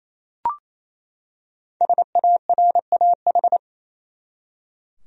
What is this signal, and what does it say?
Morse code, heard as a keyed tone, about 28 wpm, 700 Hz. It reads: SARAH
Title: USA Common Names - Max 5 Letters - 28wpm